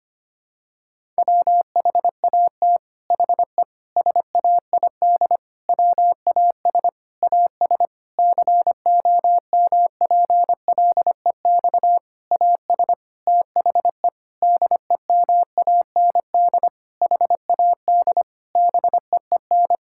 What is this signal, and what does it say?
Morse code, heard as a keyed tone, about 25 wpm, 700 Hz. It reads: WHAT HE SAID WAS AS COMPLEX AS THE DEMAND HAD BEEN
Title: Sentences from Top 1000 Words - Review Part 6 - 25wpm